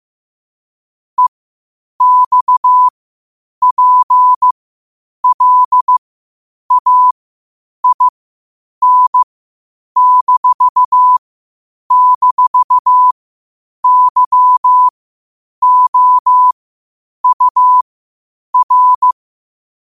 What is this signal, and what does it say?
Morse code, heard as a keyed tone, about 15 wpm, spaced as for 8 wpm, 1000 Hz. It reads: EXPLAIN--YOUR
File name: qotdfiles / 8wpm